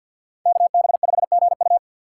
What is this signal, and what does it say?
Morse code, heard as a keyed tone, about 51 wpm, 700 Hz. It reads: KB5GU
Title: USA Call Signs - 6x Character Spacing - 50wpm